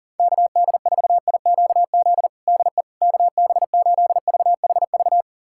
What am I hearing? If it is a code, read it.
Morse code, heard as a keyed tone, about 40 wpm, 700 Hz. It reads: KD4IQZ DE KB8VHV
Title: USA Callsign DE Callsign - 6 Characters - 40wpm